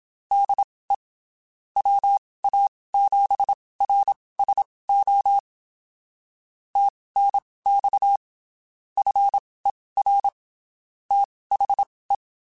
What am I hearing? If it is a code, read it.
Morse code, heard as a keyed tone, about 27 wpm, spaced as for 19 wpm, 800 Hz. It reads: DE WA7RSO TNX FER THE
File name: (spaced) / QSOs Spaced-mp3s / kc7sac @25 sp